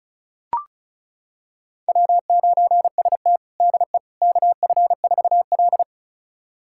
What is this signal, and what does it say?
Morse code, heard as a keyed tone, about 34 wpm, 700 Hz. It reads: W9ST DE KF4L
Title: USA Callsign DE Callsign - 4 Characters - 35wpm